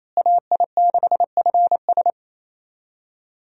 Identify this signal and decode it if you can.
Morse code, heard as a keyed tone, about 28 wpm, 700 Hz. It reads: AI6FS
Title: USA Call Signs - 5 Characters - Repeated 3x - 28wpm